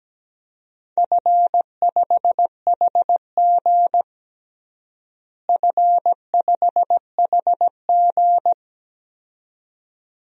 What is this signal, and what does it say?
Morse code, heard as a keyed tone, about 17 wpm, 700 Hz. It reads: F5HG F5HG
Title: DXCC-100 Call Signs - 4 Characters - Repeated 2x - 17wpm